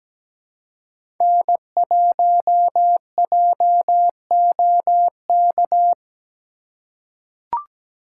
Morse code, heard as a keyed tone, about 17 wpm, 700 Hz. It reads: N1JOK
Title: USA Call Signs - 5 Characters - 2x Character Spacing - 17wpm